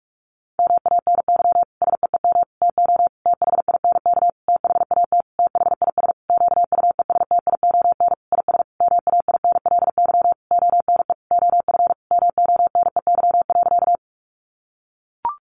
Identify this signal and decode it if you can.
Morse code, heard as a keyed tone, about 44 wpm, 700 Hz. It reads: MANY SEEM TO THINK THAT THIS QUESTION IS MAINLY ONE OF MONEY.
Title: Sentences from Top 1400 Words 45wpm